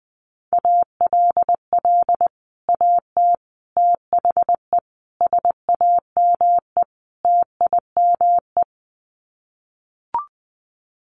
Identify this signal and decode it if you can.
Morse code, heard as a keyed tone, about 20 wpm, 700 Hz. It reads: ALL AT THE SAME TIME
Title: Sets of 5 Words Encore2 20wpm